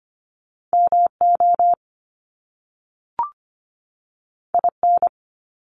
Morse code, heard as a keyed tone, about 25 wpm, 700 Hz. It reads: MO IN
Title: US States 25wpm